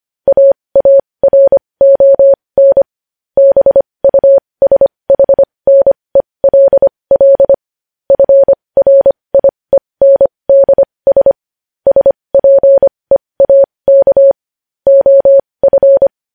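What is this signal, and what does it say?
Morse code, heard as a keyed tone, about 25 wpm, spaced as for 20 wpm, 550 Hz. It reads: AARON BUSHNELL FRIENDS SPEAK OF
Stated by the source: News headlines sent in Morse code at 20.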